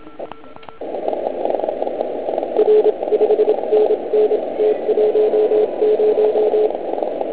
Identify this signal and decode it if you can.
Morse code, heard as a keyed tone, about 27 wpm, 430 Hz, with noise. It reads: R 5NN T10